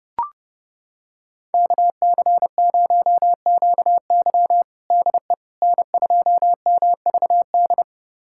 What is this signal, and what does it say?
Morse code, heard as a keyed tone, about 30 wpm, 700 Hz. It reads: KC0QY DE N2MVD
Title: USA Callsign DE Callsign - 5 Characters - 30wpm